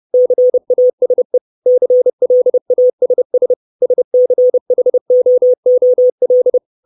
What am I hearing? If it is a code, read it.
Morse code, heard as a keyed tone, about 30 wpm, 500 Hz. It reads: CASE CLASS SCHOOL
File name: 632 Words 30WPM